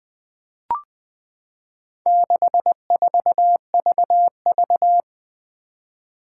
Morse code, heard as a keyed tone, about 20 wpm, 700 Hz. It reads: 64VV
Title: Four Challenging Letter-Number Combinations 20wpm